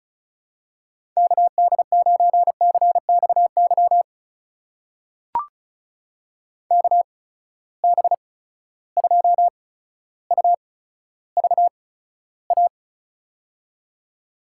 Morse code, heard as a keyed tone, about 36 wpm, spaced as for 15 wpm, 700 Hz. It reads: KD9CXY KB2UVA
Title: USA Call Signs - 6 Characters - 8x Character Spacing - 35wpm